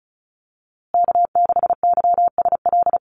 Morse code, heard as a keyed tone, about 34 wpm, 700 Hz. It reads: K6YSL
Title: USA Call Signs Repeated 2x 35wpm